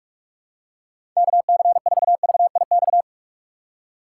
Morse code, heard as a keyed tone, about 44 wpm, 700 Hz. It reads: KC4VIX
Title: USA Call Signs - 6x Character Spacing - 45wpm